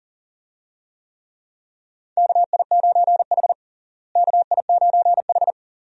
Morse code, heard as a keyed tone, about 40 wpm, 700 Hz. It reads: KI9H KI9H
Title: USA Call Signs - 4 Characters - Repeated 2x - 40wpm